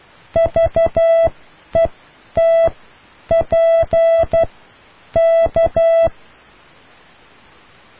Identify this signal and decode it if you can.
Morse code, heard as a keyed tone, about 12 wpm, spaced as for 9 wpm, 660 Hz. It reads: VETP K